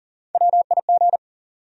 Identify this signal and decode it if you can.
Morse code, heard as a keyed tone, about 40 wpm, 700 Hz. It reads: WIG
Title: USA Call Sign Postfixes - 3 Letters 40wpm